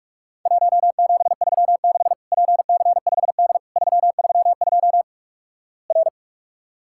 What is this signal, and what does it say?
Morse code, heard as a keyed tone, about 44 wpm, 700 Hz. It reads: 1736 PC5D 332 R
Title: Worked All Europe DX Contest - QTC Exchange - 45wpm